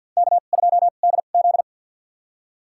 Morse code, heard as a keyed tone, about 52 wpm, spaced as for 36 wpm, 690 Hz. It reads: K2DZ
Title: USA Call Signs - 2x Character Spacing - 50wpm